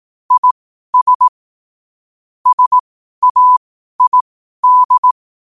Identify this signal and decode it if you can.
Morse code, heard as a keyed tone, about 18 wpm, spaced as for 12 wpm, 1000 Hz. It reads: IS SAID
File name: qotdfiles / 12wpm